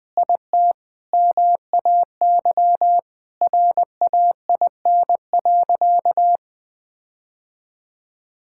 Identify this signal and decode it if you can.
Morse code, heard as a keyed tone, about 21 wpm, 700 Hz. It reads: IT MAY RAIN.
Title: Sentences with 8-Word Phrases - 20wpm